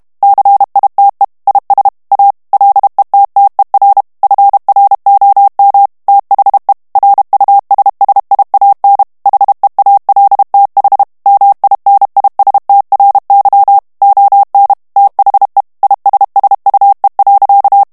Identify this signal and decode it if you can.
Morse code, heard as a keyed tone, about 32 wpm, 800 Hz. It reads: CITE IS A LETTER FROM THE RUSSIAN HEALTH MINISTRY ON THE ISSUE.
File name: ANGLAIS texte clair 2 vitesse 30 mots minute